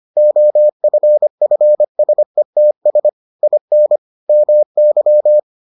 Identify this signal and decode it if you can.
Morse code, heard as a keyed tone, about 25 wpm, 600 Hz. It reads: OFFSETS IN MY